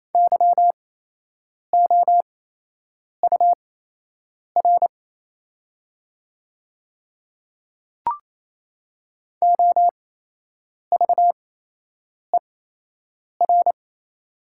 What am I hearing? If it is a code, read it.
Morse code, heard as a keyed tone, about 27 wpm, spaced as for 8 wpm, 690 Hz. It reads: YOUR OVER
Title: 8x Character Spacing - Top 300 Words - 28wpm